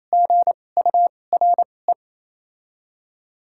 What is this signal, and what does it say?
Morse code, heard as a keyed tone, about 28 wpm, spaced as for 20 wpm, 700 Hz. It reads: GURE
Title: 2x Character Spacing - Top 500 Words - 28wpm